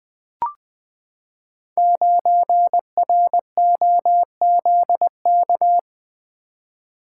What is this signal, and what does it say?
Morse code, heard as a keyed tone, about 20 wpm, 700 Hz. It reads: 9ROZK
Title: Code Groups - 5 Letter:Numbers - 20wpm